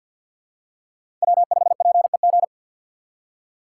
Morse code, heard as a keyed tone, about 47 wpm, 690 Hz. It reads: W5PEG
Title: USA Call Signs - 5 Characters - 3x Character Spacing - 50wpm